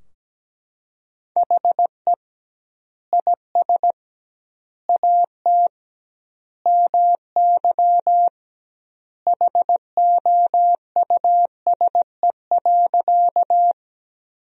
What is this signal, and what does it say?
Morse code, heard as a keyed tone, about 17 wpm, 700 Hz. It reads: HE IS AT MY HOUSE.
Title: 2x Word-Spacing - Sentences from Top 200 Words - 17wpm